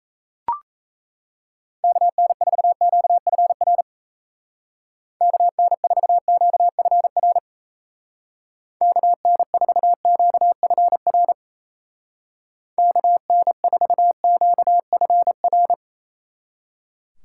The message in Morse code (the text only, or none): KN4QFR KN4QFR KN4QFR KN4QFR